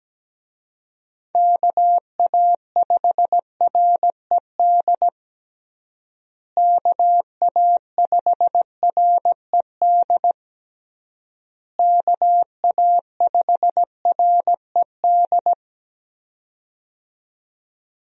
Morse code, heard as a keyed tone, about 17 wpm, 700 Hz. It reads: KA5RED KA5RED KA5RED
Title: USA Call Signs - 6 Characters - Repeated 3x - 17wpm